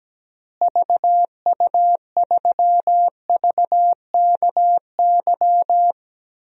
Morse code, heard as a keyed tone, about 17 wpm, 700 Hz. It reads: VU3VKY
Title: DXCC-100 Call Signs - 6 Characters - Repeated 3x - 17wpm